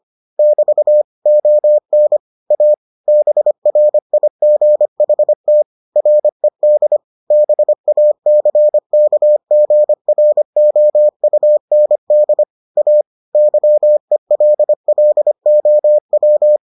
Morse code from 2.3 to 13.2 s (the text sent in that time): A BRIGHT RED BACKGROUND A